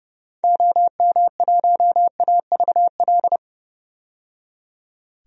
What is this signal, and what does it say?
Morse code, heard as a keyed tone, about 30 wpm, 700 Hz. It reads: OM1AVL
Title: DXCC-100 Call Signs - 6 Characters - 30wpm